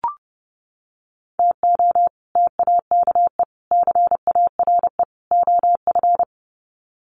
Morse code, heard as a keyed tone, about 28 wpm, 700 Hz. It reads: TO TAKE CARE OF
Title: Sets of 4 Words 30wpm